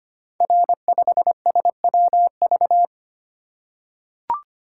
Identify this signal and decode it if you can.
Morse code, heard as a keyed tone, about 25 wpm, 700 Hz. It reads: R5SWV
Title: Code Groups - 5 Letter:Numbers - 25wpm